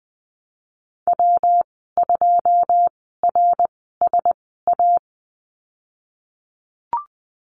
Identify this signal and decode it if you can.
Morse code, heard as a keyed tone, about 19 wpm, spaced as for 14 wpm, 700 Hz. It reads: W2RSA
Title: 140 Lesson 23g - Letter f - Call Signs 20wpm